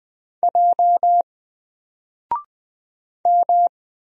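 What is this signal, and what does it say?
Morse code, heard as a keyed tone, about 20 wpm, 700 Hz. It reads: J M
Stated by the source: Single letter and number Morse drill at 20 wpm, from N4LCD.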